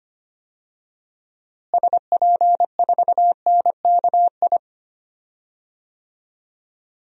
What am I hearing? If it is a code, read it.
Morse code, heard as a keyed tone, about 25 wpm, 700 Hz. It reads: SP4NKI
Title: DXCC-100 Call Signs - 6 Characters - Repeated 2x - 25wpm